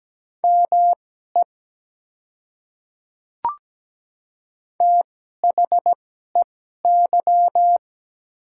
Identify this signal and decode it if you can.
Morse code, heard as a keyed tone, about 17 wpm, spaced as for 12 wpm, 700 Hz. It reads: ME THEY
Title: 2x Character Spacing - Top 200 Words - 17wpm